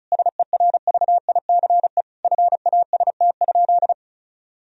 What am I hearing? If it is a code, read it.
Morse code, heard as a keyed tone, about 34 wpm, 700 Hz. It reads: SERVICE FAST?
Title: Sentences from Top 1000 Words - Review Part 4 - 35wpm